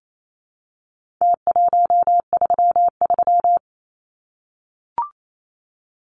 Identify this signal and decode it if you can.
Morse code, heard as a keyed tone, about 28 wpm, 700 Hz. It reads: T133
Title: Numbers - Contest Style - 4-Digit 28wpm